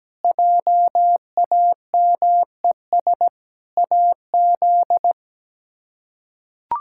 Morse code, heard as a keyed tone, about 17 wpm, 700 Hz. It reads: JAMES AZ
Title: K1USN SST QSO Practice - 17wpm